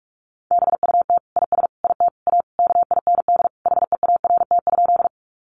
Morse code, heard as a keyed tone, about 44 wpm, 700 Hz. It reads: BUT IS IT A KIND HEART?
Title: Sentences from Top 1200 Words 45wpm